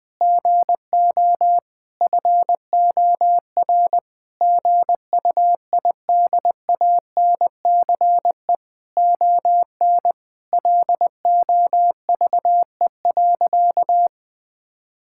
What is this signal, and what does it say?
Morse code, heard as a keyed tone, about 20 wpm, 700 Hz. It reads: GO FOR GUIDANCE ON LOVE.
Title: Sentences from Top 1600 Words 20wpm